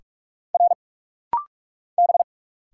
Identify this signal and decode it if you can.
Morse code, heard as a keyed tone, about 46 wpm, 700 Hz. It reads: R B
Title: Single Letters - Rapid-Fire 45wpm